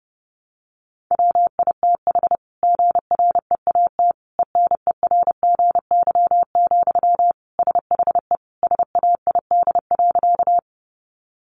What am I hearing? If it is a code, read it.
Morse code, heard as a keyed tone, about 30 wpm, 700 Hz. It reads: WITH GREAT ENERGY, SHE SAID.
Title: Sentences from Top 900 Words 30wpm